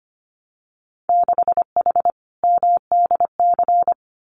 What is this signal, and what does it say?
Morse code, heard as a keyed tone, about 25 wpm, 700 Hz. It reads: 6H MDC
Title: Winter Field Day Exchanges Encore 25wpm